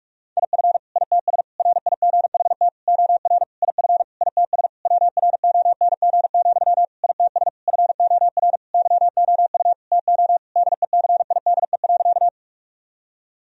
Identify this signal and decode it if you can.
Morse code, heard as a keyed tone, about 41 wpm, 700 Hz. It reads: IF ITS RIGHT OR IF ITS WRONG, ITS FOR YOU TO DECIDE.
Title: Sentences from Top 1000 Words - Review Part 3 - 45wpm